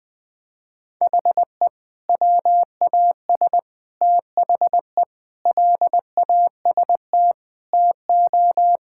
Morse code, heard as a keyed tone, about 21 wpm, 700 Hz. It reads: HE WAS THE LAST TO
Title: Sentences with 2-Word Phrases - Spoken Beginning - 20wpm